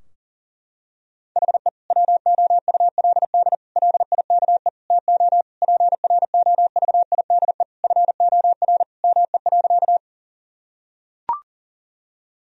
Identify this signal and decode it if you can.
Morse code, heard as a keyed tone, about 40 wpm, 700 Hz. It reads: HE WOULD LIKE TO PROVIDE FOR ME.
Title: Sentences from Top 1000 Words - Review Part 3 - 40wpm